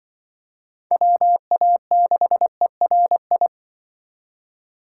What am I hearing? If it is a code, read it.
Morse code, heard as a keyed tone, about 25 wpm, 700 Hz. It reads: WA6ERI